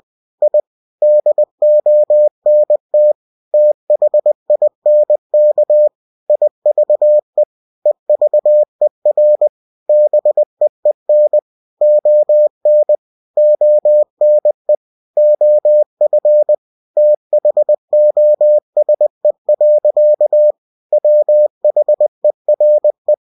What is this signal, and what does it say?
Morse code, heard as a keyed tone, about 20 wpm, 600 Hz. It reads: I DONT THINK IVE EVER BEEN ON ONE OF THOSE. WHERE